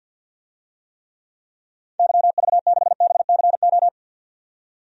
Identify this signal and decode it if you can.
Morse code, heard as a keyed tone, about 50 wpm, 700 Hz. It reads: YV6BCO
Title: DXCC-100 Call Signs - 6 Characters - 50wpm